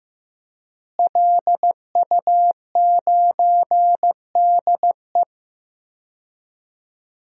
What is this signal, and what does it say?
Morse code, heard as a keyed tone, about 16 wpm, 700 Hz. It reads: LU9DE
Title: DXCC-100 Call Signs - 5 Characters - Repeated 3x - 15wpm